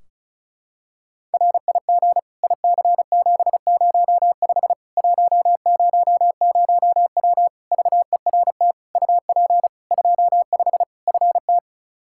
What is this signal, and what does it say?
Morse code, heard as a keyed tone, about 34 wpm, 700 Hz. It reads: RIG IC705 100W VERT UP 25 FT
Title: K1USN SST QSO Practice 35wpm